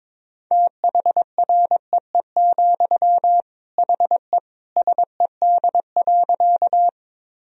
Morse code, heard as a keyed tone, about 21 wpm, 700 Hz. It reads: THREE, HE SED.
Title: Sentences from Top 200 Words and Abbreviations 22wpm